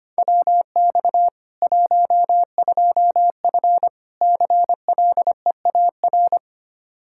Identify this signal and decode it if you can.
Morse code, heard as a keyed tone, about 25 wpm, 700 Hz. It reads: WX 12F CLEAR
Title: Practice QSO - Exchange 2 - Info - 25wpm